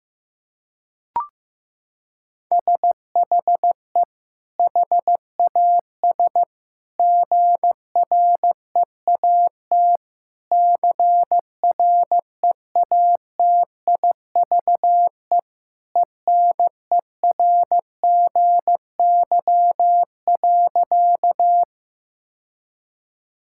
Morse code, heard as a keyed tone, about 15 wpm, 700 Hz. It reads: SHE HAS GREAT CREATIVE ENERGY.
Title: Sentences from Top 1000 Words - Review Part 6 - 15wpm